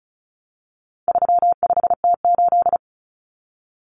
Morse code, heard as a keyed tone, about 34 wpm, 700 Hz. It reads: 35T8
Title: Common Cut Numbers - 4-Digit 35wpm